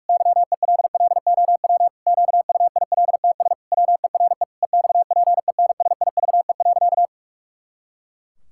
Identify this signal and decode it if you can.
Morse code, heard as a keyed tone, about 46 wpm, 700 Hz. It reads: YELLOW QUILTS WERE EXPENSIVE.
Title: Pangram Challenge v2 - 45wpm